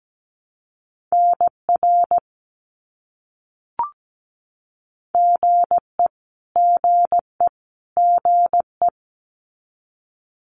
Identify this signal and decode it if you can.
Morse code, heard as a keyed tone, about 17 wpm, 700 Hz. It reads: NR GE GE GE
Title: Abbreviations Repeated 3x - 17wpm